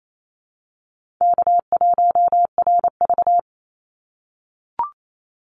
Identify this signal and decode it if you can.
Morse code, heard as a keyed tone, about 28 wpm, 700 Hz. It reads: K1RV
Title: Easy Call Signs - 28wpm